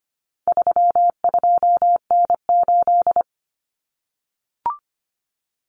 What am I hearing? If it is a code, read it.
Morse code, heard as a keyed tone, about 25 wpm, 700 Hz. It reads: 32N8
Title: Common Cut Numbers - 4-Digit 25wpm